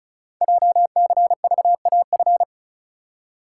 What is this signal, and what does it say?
Morse code, heard as a keyed tone, about 36 wpm, 700 Hz. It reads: JCVAF
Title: Code Groups - 5 Letters - 35wpm